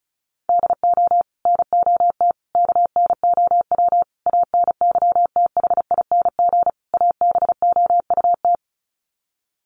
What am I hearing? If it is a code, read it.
Morse code, heard as a keyed tone, about 34 wpm, 700 Hz. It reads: DO NOT KNOW ANYTHING ABOUT
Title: Sets of 5 Words 35wpm